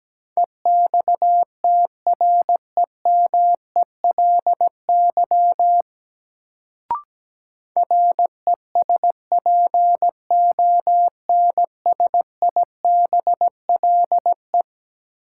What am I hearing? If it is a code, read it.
Morse code, heard as a keyed tone, about 17 wpm, 700 Hz. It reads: EXTREMELY RESPONSIBLE
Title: Sentences from Top 1000 Words - Review Part 7 - New Word Drill - 17wpm